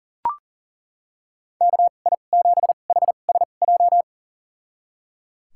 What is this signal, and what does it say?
Morse code, heard as a keyed tone, about 37 wpm, spaced as for 28 wpm, 700 Hz. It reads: KI7HSJ
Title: USA Call Signs - 6 Characters - 2x Character Spacing - 40wpm